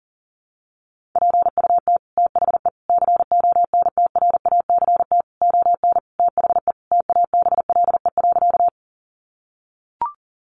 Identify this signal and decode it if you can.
Morse code, heard as a keyed tone, about 40 wpm, 700 Hz. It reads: PUT THE CONTRACT ON THE TABLE.
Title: Sentences from Top 1400 Words 40wpm